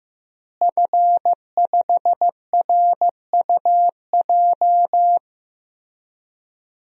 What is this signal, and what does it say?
Morse code, heard as a keyed tone, about 15 wpm, 700 Hz. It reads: F5RUJ
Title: DXCC-100 Call Signs - 5 Characters - Repeated 3x - 15wpm